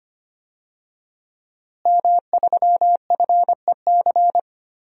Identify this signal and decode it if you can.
Morse code, heard as a keyed tone, about 25 wpm, 700 Hz. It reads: M3FEC
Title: DXCC-100 Call Signs - 5 Characters - Repeated 2x - 25wpm